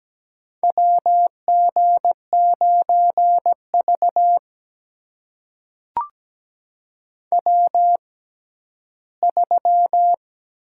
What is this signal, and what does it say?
Morse code, heard as a keyed tone, about 17 wpm, spaced as for 12 wpm, 700 Hz. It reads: WG9V W3
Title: USA Call Signs - 4 Characters - 6x Character Spacing - 17wpm